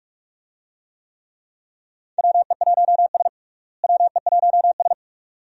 Morse code, heard as a keyed tone, about 46 wpm, 700 Hz. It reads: WE1S WE1S
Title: USA Call Signs - 4 Characters - Repeated 2x - 45wpm